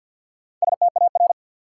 Morse code, heard as a keyed tone, about 52 wpm, 700 Hz. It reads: STAR
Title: Sentences from Top 1000 Words - Review Part 2 - New Word Drill - 50wpm